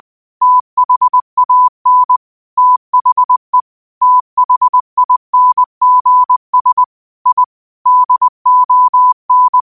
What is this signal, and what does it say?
Morse code, heard as a keyed tone, about 20 wpm, 1000 Hz. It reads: THAN THE THINGS I DON
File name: qotdfiles / 20wpm